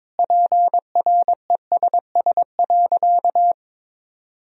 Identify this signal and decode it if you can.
Morse code, heard as a keyed tone, about 22 wpm, 700 Hz. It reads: PRESS.
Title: Sentences from Top 1000 Words - Review Part 1 - 22wpm